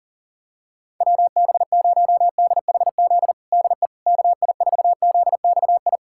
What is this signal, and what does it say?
Morse code, heard as a keyed tone, about 40 wpm, 700 Hz. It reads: WB0DHZ DE KI4ZXI
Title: USA Callsign DE Callsign - 6 Characters - 40wpm